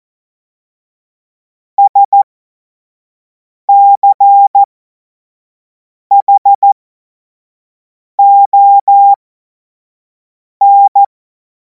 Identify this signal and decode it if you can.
Morse code, heard as a keyed tone, about 14 wpm, spaced as for 5 wpm, 800 Hz. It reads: SCHON
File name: Morsekurs / Witze / Witz 17. Februar 2026 25 F70